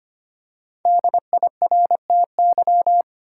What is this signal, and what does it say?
Morse code, heard as a keyed tone, about 25 wpm, 700 Hz. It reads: DIRTY
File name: 5 Letter Words - First 3 Letters Given - 25wpm